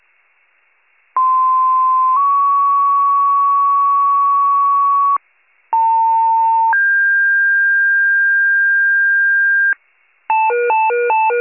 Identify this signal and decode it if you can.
Two-tone page, heard as a one-second tone, then a three-second tone, about 1040 then 1120 Hz, about 900 then 1600 Hz. It is heard from 1.2 to 9.7 s.
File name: 95 RESCUE